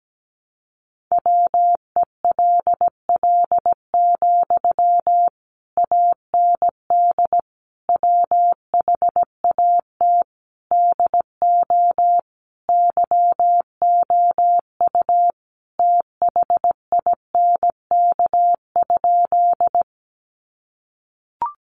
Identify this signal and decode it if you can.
Morse code, heard as a keyed tone, about 17 wpm, 700 Hz. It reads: WELL, AND WHAT DO YOU THINK?
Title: Sentences from Top 200 Words 17wpm